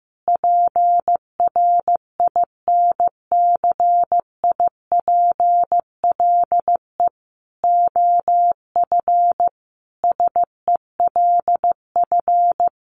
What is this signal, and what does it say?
Morse code, heard as a keyed tone, about 15 wpm, 700 Hz. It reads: PRINCIPLE OF SELF
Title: Sentences from Top 1400 Words 15wpm